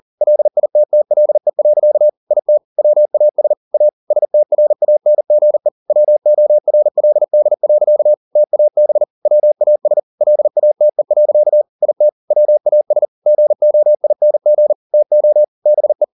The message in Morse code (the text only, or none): LITTLE. IT WAS A STRANGE WORLD. TAB WAS LATE. IT WAS GOING TO BE